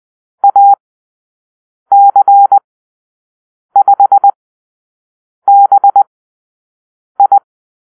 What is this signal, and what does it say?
Morse code, heard as a keyed tone, about 20 wpm, spaced as for 6 wpm, 800 Hz. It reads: AC5BI
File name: arrl / Call-Sign Drill-6